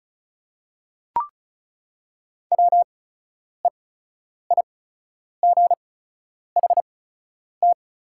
Morse code, heard as a keyed tone, about 34 wpm, spaced as for 10 wpm, 700 Hz. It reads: WEIGHT
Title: 8x Character Spacing - Top 500 Words - 35wpm